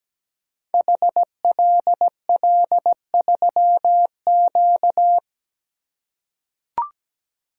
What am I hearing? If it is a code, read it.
Morse code, heard as a keyed tone, about 17 wpm, 700 Hz. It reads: HLL3Q
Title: Code Groups - 5 Letter:Numbers - 17wpm